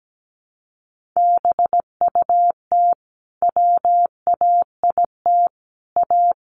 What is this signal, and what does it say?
Morse code, heard as a keyed tone, about 17 wpm, 700 Hz. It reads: BUT WAIT A
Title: Sentences from Top 500 Words 17wpm